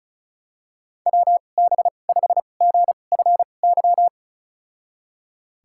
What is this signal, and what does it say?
Morse code, heard as a keyed tone, about 36 wpm, spaced as for 25 wpm, 700 Hz. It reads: WB5GFY
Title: USA Call Signs - 6 Characters - 2x Character Spacing - 35wpm